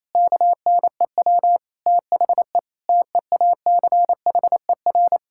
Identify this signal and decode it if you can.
Morse code, heard as a keyed tone, about 28 wpm, 700 Hz. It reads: KNEW THE TEACHER
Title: Sentences from Top 1000 Words - Review Part 6 - 28wpm